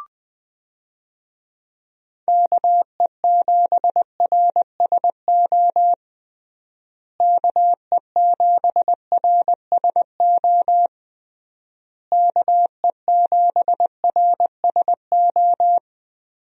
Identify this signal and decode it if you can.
Morse code, heard as a keyed tone, about 21 wpm, 700 Hz. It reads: KE7RSO KE7RSO KE7RSO
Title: USA Call Signs - 6 Characters - Repeated 3x - 20wpm